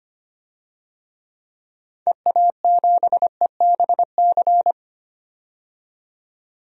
Morse code, heard as a keyed tone, about 25 wpm, 700 Hz. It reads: EA7EBC
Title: DXCC-100 Call Signs - 6 Characters - 25wpm